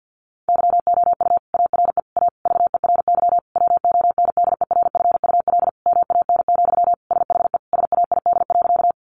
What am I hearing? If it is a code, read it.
Morse code, heard as a keyed tone, about 50 wpm, 700 Hz. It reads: YOU ARE A VERY WONDERFUL MAN, SHE SAID.
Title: Sentences from Top 1100 Words 50wpm